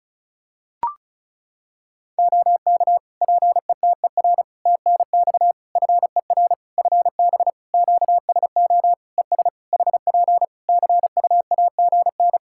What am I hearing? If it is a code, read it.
Morse code, heard as a keyed tone, about 36 wpm, 700 Hz. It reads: OK PETER TNX FER FB QSO ES HP CUAGN
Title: Practice QSO - Exchange 4 - Ending - 35wpm